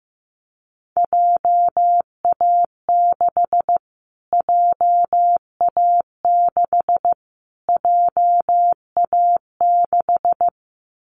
Morse code, heard as a keyed tone, about 15 wpm, 700 Hz. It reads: JA6 JA6 JA6
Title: Callsign Prefixes for Top 25 DXCC - Repeated 3x - 15wpm